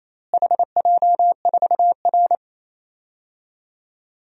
Morse code, heard as a keyed tone, about 28 wpm, 700 Hz. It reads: HJ4R
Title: DXCC-100 Call Signs - 4 Characters - Repeated 2x - 28wpm